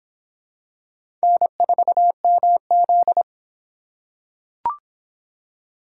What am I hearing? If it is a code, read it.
Morse code, heard as a keyed tone, about 24 wpm, 700 Hz. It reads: N4MZ